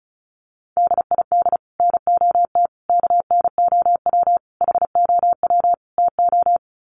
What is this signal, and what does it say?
Morse code, heard as a keyed tone, about 34 wpm, 700 Hz. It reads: DID NOT KNOW HOW TO
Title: Sets of 5 Words 35wpm